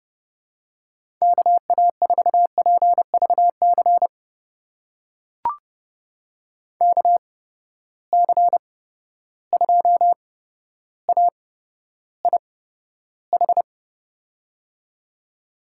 Morse code, heard as a keyed tone, about 31 wpm, spaced as for 13 wpm, 700 Hz. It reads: KA4PVC KC2AIH
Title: USA Call Signs - 8x Character Spacing - 30wpm